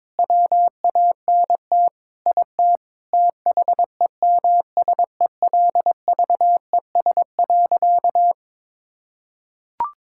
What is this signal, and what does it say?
Morse code, heard as a keyed tone, about 22 wpm, 700 Hz. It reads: WANT IT THEMSELVES.
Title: Sentences from Top 1000 Words - Review Part 5 - 22wpm